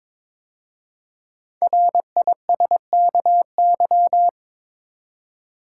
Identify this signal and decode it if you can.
Morse code, heard as a keyed tone, about 22 wpm, 700 Hz. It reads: RISKY
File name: 5 Letter Words - First 3 Letters Given - 22wpm